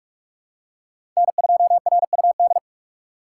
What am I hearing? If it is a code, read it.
Morse code, heard as a keyed tone, about 45 wpm, 700 Hz. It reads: N2RUD
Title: USA Call Signs - 3x Character Spacing - 45wpm